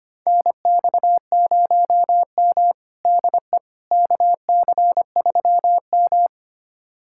Morse code, heard as a keyed tone, about 25 wpm, 700 Hz. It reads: NX0M DE KC3M
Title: USA Callsign DE Callsign - 4 Characters - 25wpm